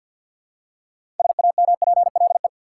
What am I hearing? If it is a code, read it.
Morse code, heard as a keyed tone, about 52 wpm, 700 Hz. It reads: SAMPLE
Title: Six Letter Words - 50wpm